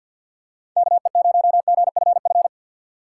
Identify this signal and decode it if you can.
Morse code, heard as a keyed tone, about 47 wpm, 700 Hz. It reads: KE0GFF
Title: USA Call Signs - 4x Character Spacing - 50wpm